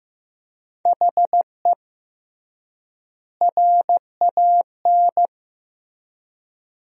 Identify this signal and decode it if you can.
Morse code, heard as a keyed tone, about 15 wpm, 700 Hz. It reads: HE RAN
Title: 3x Word-Spacing - Sentences from Top 500 Words - 15wpm